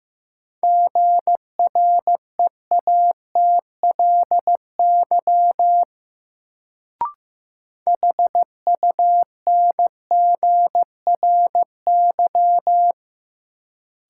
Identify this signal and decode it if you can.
Morse code, heard as a keyed tone, about 15 wpm, 700 Hz. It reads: GREATLY HUNGRY
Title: Sentences from Top 1400 Words - New Word Drill - 15wpm